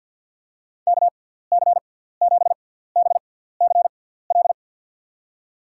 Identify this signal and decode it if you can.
Morse code, heard as a keyed tone, about 53 wpm, spaced as for 17 wpm, 700 Hz. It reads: KC7BCL
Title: USA Call Signs - 6 Characters - 6x Character Spacing - 50wpm